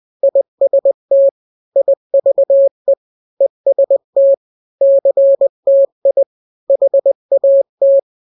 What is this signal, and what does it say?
Morse code, heard as a keyed tone, about 20 wpm, 550 Hz. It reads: IST IVE EST CTI HAT
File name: n-grams / 3-grams 20 18 WPM